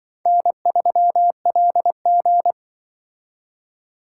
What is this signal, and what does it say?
Morse code, heard as a keyed tone, about 24 wpm, 700 Hz. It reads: N3LG
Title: Speed Racing - USA 1x2 Call Signs 20wpm